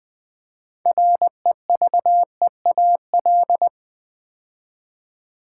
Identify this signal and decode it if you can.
Morse code, heard as a keyed tone, about 20 wpm, 700 Hz. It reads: REVEAL
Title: Build A Word - 3 Letters At A Time - 6-Letter Words - 20wpm